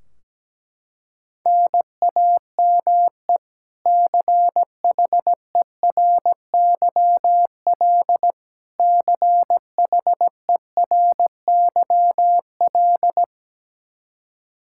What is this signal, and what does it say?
Morse code, heard as a keyed tone, about 17 wpm, 700 Hz. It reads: NAME CHERYL CHERYL
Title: K1USN SST QSO Practice 17wpm